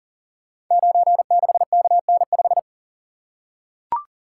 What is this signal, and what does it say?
Morse code, heard as a keyed tone, about 38 wpm, 700 Hz. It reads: 96KN5
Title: Code Groups - 5 Letter:Numbers - 40wpm